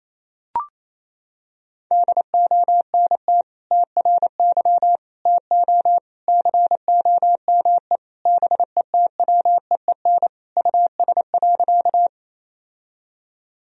Morse code, heard as a keyed tone, about 28 wpm, 700 Hz. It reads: DONT TRY TO COME BETWEEN US.
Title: Sentences from Top 1000 Words - Review Part 1 - 28wpm